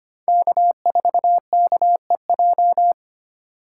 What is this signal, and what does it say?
Morse code, heard as a keyed tone, about 25 wpm, 700 Hz. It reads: K4KEJ
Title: USA Call Signs - 5 Characters - 2x Character Spacing - 25wpm